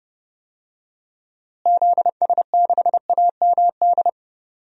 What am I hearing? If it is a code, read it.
Morse code, heard as a keyed tone, about 30 wpm, 700 Hz. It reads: ZS6AMD
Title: DXCC-100 Call Signs - 6 Characters - Repeated 2x - 30wpm